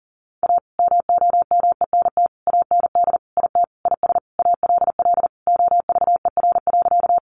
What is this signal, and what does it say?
Morse code, heard as a keyed tone, about 40 wpm, 700 Hz. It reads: A MOMENT AND IT IS ALL OVER.
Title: Sentences from Top 1000 Words 40wpm